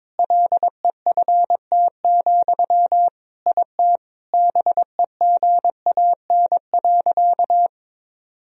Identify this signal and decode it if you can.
Morse code, heard as a keyed tone, about 22 wpm, 700 Hz. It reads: LEFT, IT BEGAN.
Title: Sentences from Top 400 Words and Abbreviations 22wpm